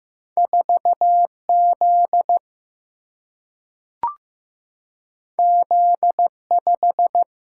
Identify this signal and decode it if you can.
Morse code, heard as a keyed tone, about 15 wpm, 700 Hz. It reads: 4 Z Z 5
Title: Single-Letter Single-Number 15wpm